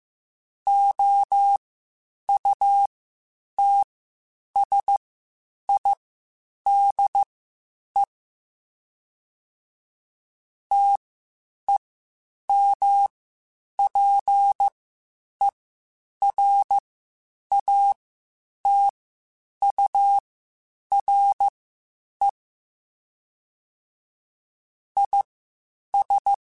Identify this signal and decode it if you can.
Morse code, heard as a keyed tone, about 15 wpm, spaced as for 8 wpm, 800 Hz. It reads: OUTSIDE TEMPERATURE IS
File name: Spaced Characters / Spaced Characters-mp3s / Weather 4 14 sp @13